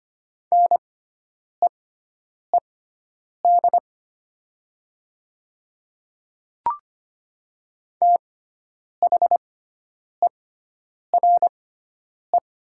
Morse code, heard as a keyed tone, about 25 wpm, spaced as for 9 wpm, 690 Hz. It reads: NEED THERE
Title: 6x Character Spacing - Top 200 Words - 25wpm